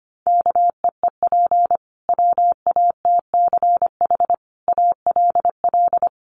Morse code, heard as a keyed tone, about 25 wpm, 700 Hz. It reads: KEEP WATCH ALL
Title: Sentences from Top 300 Words 25wpm